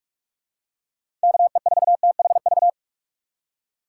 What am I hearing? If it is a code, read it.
Morse code, heard as a keyed tone, about 46 wpm, 700 Hz. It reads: K E 4 T H V
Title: USA Call Signs - 6 Characters - 6x Character Spacing - 45wpm